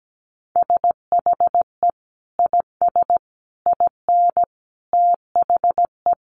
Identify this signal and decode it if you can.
Morse code, heard as a keyed tone, about 17 wpm, 700 Hz. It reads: SHE IS IN THE
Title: Sentences from Top 1200 Words 17wpm